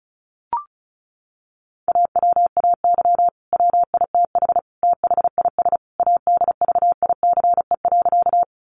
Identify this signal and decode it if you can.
Morse code, heard as a keyed tone, about 34 wpm, 700 Hz. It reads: AWAY WITH THIS ADVICE.
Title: Sentences from Top 1100 Words 35wpm